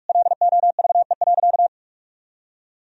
Morse code, heard as a keyed tone, about 46 wpm, 700 Hz. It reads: LOVE.
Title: Sentences from Top 1000 Words - Review Part 4 - 45wpm